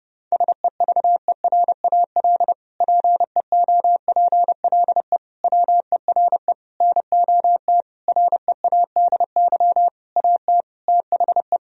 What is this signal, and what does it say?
Morse code, heard as a keyed tone, about 30 wpm, 700 Hz. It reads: SEVERAL PEOPLE WERE NOT READY AT THE
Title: Sentences from Top 1000 Words - Review Part 2 - 30wpm